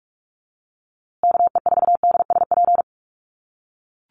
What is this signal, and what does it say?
Morse code, heard as a keyed tone, about 44 wpm, 700 Hz. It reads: KE4DSP
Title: USA Call Signs Repeated 3x 45wpm